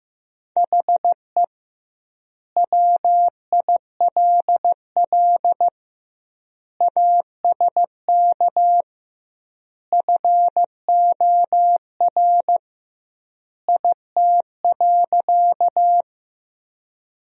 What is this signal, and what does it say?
Morse code, heard as a keyed tone, about 15 wpm, 700 Hz. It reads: HE WILL ASK FOR IT.
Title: 2x Word-Spacing - Sentences from Top 200 Words - 15wpm